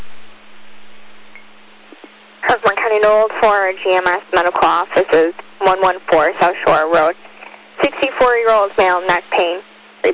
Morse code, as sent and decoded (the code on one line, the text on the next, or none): none